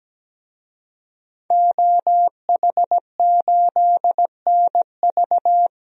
Morse code, heard as a keyed tone, about 17 wpm, 700 Hz. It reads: OH8NV
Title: DXCC-100 Call Signs - 5 Characters - Repeated 3x - 17wpm